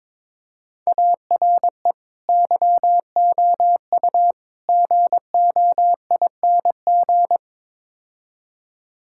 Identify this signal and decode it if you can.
Morse code, heard as a keyed tone, about 22 wpm, 700 Hz. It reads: ARE YOU GOING
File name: practice / 22wpm / Build A Phrase - 5-Word Phrases - 22wpm